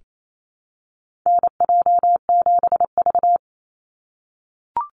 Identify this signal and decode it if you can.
Morse code, heard as a keyed tone, about 28 wpm, 700 Hz. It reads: NJ7V
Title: NA SOTA Activator Callsigns 28wpm